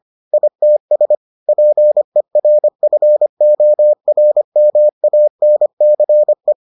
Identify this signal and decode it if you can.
Morse code, heard as a keyed tone, about 25 wpm, 600 Hz. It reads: ITS PERFORMANCE